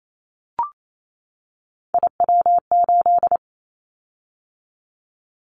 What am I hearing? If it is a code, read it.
Morse code, heard as a keyed tone, about 28 wpm, 700 Hz. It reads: IW8
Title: Callsign Prefixes for Top 25 DXCC - 28wpm